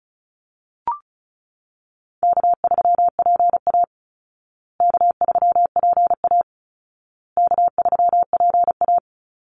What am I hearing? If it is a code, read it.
Morse code, heard as a keyed tone, about 34 wpm, 700 Hz. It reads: K3PA K3PA K3PA
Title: Call Signs Repeated 2x - 35wpm